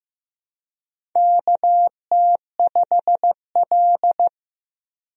Morse code, heard as a keyed tone, about 15 wpm, 700 Hz. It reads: KT5L
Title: USA Call Signs - 4 Characters - 15wpm